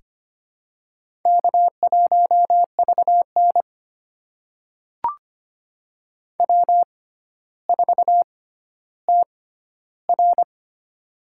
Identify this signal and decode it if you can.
Morse code, heard as a keyed tone, about 25 wpm, spaced as for 13 wpm, 700 Hz. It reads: K1VN W4TR
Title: USA Call Signs - 4 Characters - 6x Character Spacing - 25wpm